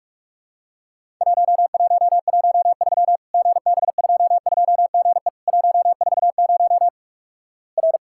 Morse code, heard as a keyed tone, about 46 wpm, 700 Hz. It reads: 1113 GB22GE 140 R
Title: Worked All Europe DX Contest - QTC Exchange - 45wpm